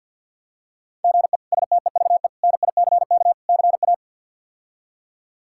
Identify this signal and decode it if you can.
Morse code, heard as a keyed tone, about 46 wpm, 700 Hz. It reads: GE STEVE NICK CA
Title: K1USN SST QSO Practice - 50wpm